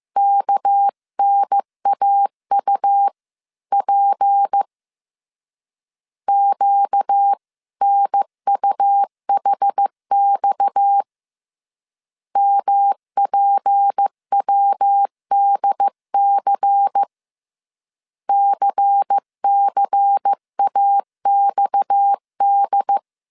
Morse code, heard as a keyed tone, about 15 wpm, 800 Hz. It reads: KNAUP QNUHX MPWDC CCAXD
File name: CW / DET / 1 6 1500C S C 14wpm 2951